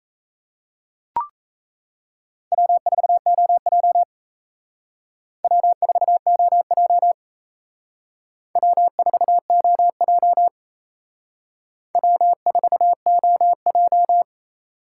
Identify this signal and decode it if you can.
Morse code, heard as a keyed tone, about 35 wpm, 700 Hz. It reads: W4OJ W4OJ W4OJ W4OJ
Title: Speed Racing - USA 1x2 Call Signs 28wpm